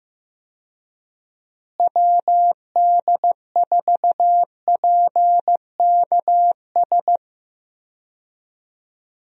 Morse code, heard as a keyed tone, about 16 wpm, 700 Hz. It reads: WD4PKS
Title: USA Call Signs - 6 Characters - Repeated 3x - 15wpm